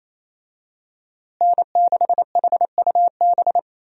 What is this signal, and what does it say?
Morse code, heard as a keyed tone, about 28 wpm, 700 Hz. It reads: N6HUB
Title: USA Call Signs - 5 Characters - Repeated 3x - 28wpm